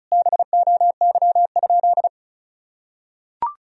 Morse code, heard as a keyed tone, about 34 wpm, 700 Hz. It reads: BOY?
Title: Sentences from Top 1000 Words - Review Part 4 - 35wpm